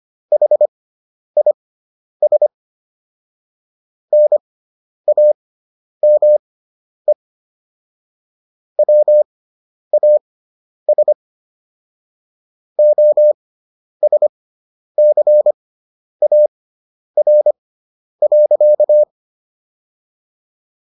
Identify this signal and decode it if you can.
Morse code, heard as a keyed tone, about 25 wpm, spaced as for 10 wpm, 600 Hz. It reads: HIS NAME WAS OSCAR.